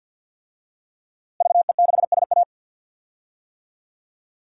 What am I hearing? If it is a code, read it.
Morse code, heard as a keyed tone, about 48 wpm, 700 Hz. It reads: VE6SA
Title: DXCC-100 Call Signs - 5 Characters - Repeated 3x - 50wpm